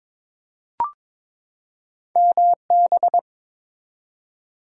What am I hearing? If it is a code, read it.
Morse code, heard as a keyed tone, about 21 wpm, 700 Hz. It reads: MB
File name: practice / 22wpm / ARRL Sections v2 - 22wpm